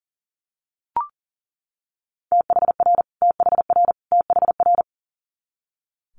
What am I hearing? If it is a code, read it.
Morse code, heard as a keyed tone, about 37 wpm, 700 Hz. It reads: THR THR THR
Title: Multi-letter Phonemes Repeated 3x 40wpm